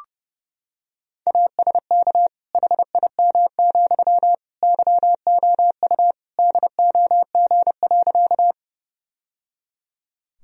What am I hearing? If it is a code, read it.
Morse code, heard as a keyed tone, about 30 wpm, 700 Hz. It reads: ASK HIM, YOU DOG.
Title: Sentences from Top 1000 Words - Review Part 2 - 30wpm